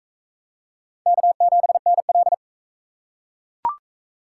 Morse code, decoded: K7NL

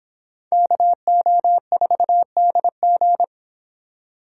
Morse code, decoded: KO4DG

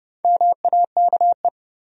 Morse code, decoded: MAKE

